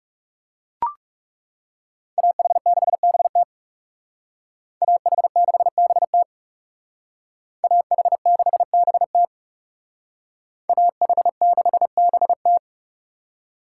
Morse code, heard as a keyed tone, about 39 wpm, 700 Hz. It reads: AH6BT AH6BT AH6BT AH6BT